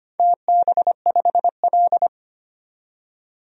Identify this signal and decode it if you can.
Morse code, heard as a keyed tone, about 25 wpm, 700 Hz. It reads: TB5L
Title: DXCC-100 Call Signs - 4 Characters - 25wpm